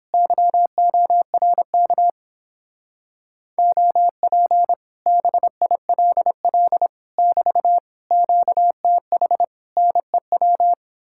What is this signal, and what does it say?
Morse code, heard as a keyed tone, about 29 wpm, 700 Hz. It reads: YORK OP BILL <BT> QTH NEW